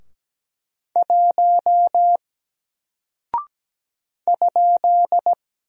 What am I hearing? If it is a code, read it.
Morse code, heard as a keyed tone, about 17 wpm, 700 Hz. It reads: 1 ?
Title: Single Character (Letter:Number:Punctuation:Prosign) 17wpm